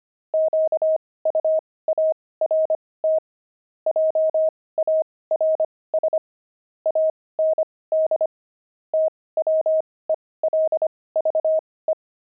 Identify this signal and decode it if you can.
Morse code, heard as a keyed tone, about 25 wpm, spaced as for 18 wpm, 630 Hz. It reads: QUART JARS AND TWELVE
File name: pangram-fs-18-cs-25-twelve